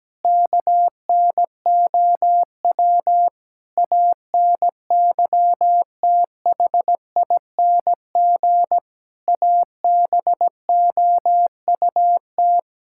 Morse code, heard as a keyed tone, about 17 wpm, 700 Hz. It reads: KNOW ANYTHING ABOUT
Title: Sentences from Top 1000 Words - Review Part 6 - 17wpm